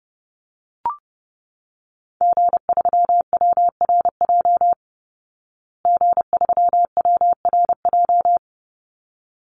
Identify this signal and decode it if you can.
Morse code, heard as a keyed tone, about 30 wpm, 700 Hz. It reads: G3WRJ G3WRJ
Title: Call Signs Repeated 3x - 30wpm